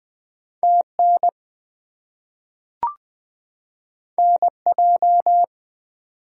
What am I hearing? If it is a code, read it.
Morse code, heard as a keyed tone, about 20 wpm, 700 Hz. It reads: TN NJ
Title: US States and Canadian Provinces 20wpm